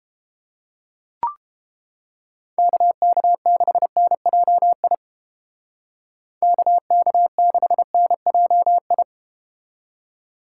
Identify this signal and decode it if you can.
Morse code, heard as a keyed tone, about 31 wpm, 700 Hz. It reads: KK6NJI KK6NJI